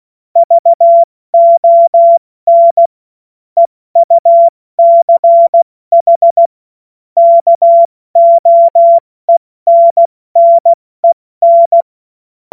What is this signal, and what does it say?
Morse code, heard as a keyed tone, about 16 wpm, 680 Hz. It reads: VON EUCH KOENNEN
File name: Morsekurs / Witze / Witz 01. August 2023 70 F80